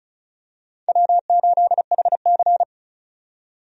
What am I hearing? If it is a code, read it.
Morse code, heard as a keyed tone, about 34 wpm, 700 Hz. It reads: W8HC
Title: USA Call Signs - 4 Characters - 35wpm